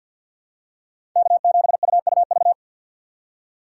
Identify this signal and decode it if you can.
Morse code, heard as a keyed tone, about 50 wpm, 700 Hz. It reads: K7UUV
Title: USA Call Signs - 5 Characters - 2x Character Spacing - 50wpm